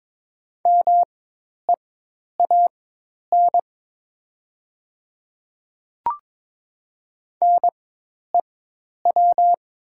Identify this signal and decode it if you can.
Morse code, heard as a keyed tone, about 22 wpm, spaced as for 10 wpm, 700 Hz. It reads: MEAN NEW
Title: 4x Character Spacing - Top 200 Words - 22wpm